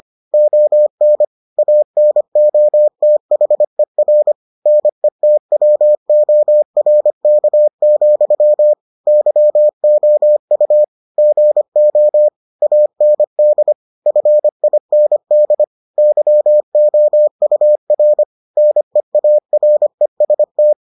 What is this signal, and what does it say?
Morse code, heard as a keyed tone, about 25 wpm, 600 Hz. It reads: ON ANOTHER NETWORK, YOU GO AND FIND YOUR NEAREST